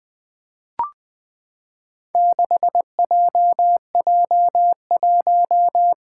Morse code, heard as a keyed tone, about 20 wpm, 700 Hz. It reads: 6JJ1
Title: Four Challenging Letter-Number Combinations 20wpm